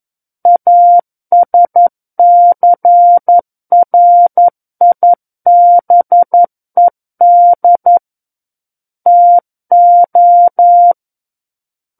Morse code, heard as a keyed tone, about 11 wpm, 700 Hz. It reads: ASCRIBED TO
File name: Book One / 02-Chapter 2 - 11 WPM